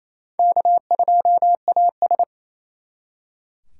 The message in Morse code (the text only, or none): K2AS